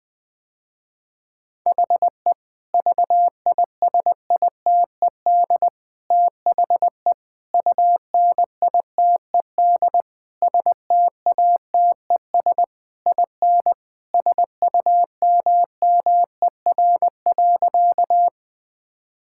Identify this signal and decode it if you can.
Morse code, heard as a keyed tone, about 21 wpm, 700 Hz. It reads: HE VISITED THE UNITED STATES IN SUMMER.
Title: Sentences with 4-Word Phrases - Spoken Beginning - 20wpm